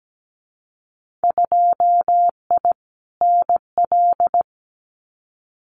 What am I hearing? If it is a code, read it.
Morse code, heard as a keyed tone, about 17 wpm, 700 Hz. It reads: 2I NL
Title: Winter Field Day Exchanges Encore 17wpm